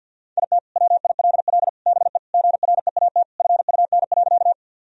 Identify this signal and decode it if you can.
Morse code, heard as a keyed tone, about 50 wpm, 700 Hz. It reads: IT WILL BE GREAT FUN.
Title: Sentences from Top 1000 Words - Review Part 4 - 50wpm